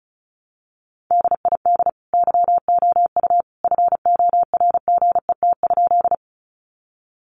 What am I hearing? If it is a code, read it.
Morse code, heard as a keyed tone, about 34 wpm, 700 Hz. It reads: DID YOU FORGET?
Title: Sentences from Top 1200 Words 35wpm